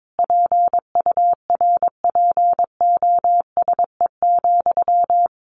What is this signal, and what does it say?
Morse code, heard as a keyed tone, about 22 wpm, 700 Hz. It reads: PURPOSE,
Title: Sentences from Top 800 Words 22wpm